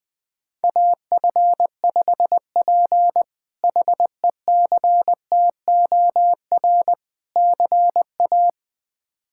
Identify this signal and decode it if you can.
Morse code, heard as a keyed tone, about 20 wpm, 700 Hz. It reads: AF5P HECTOR CA
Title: K1USN SST QSO Practice - 20wpm